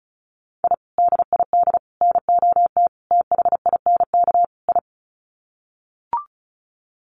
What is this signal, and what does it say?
Morse code, heard as a keyed tone, about 34 wpm, 700 Hz. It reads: I DID NOT THINK I
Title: Sets of 5 Words Encore 35wpm